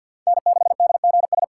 Morse code, heard as a keyed tone, about 53 wpm, 700 Hz. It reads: N6DGS
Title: USA Call Signs - 4x Character Spacing - 50wpm